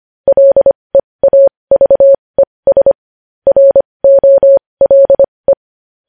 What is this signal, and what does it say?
Morse code, heard as a keyed tone, about 25 wpm, spaced as for 20 wpm, 550 Hz. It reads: LEAVES ROLE
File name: podcast / News headlines-1x-25x20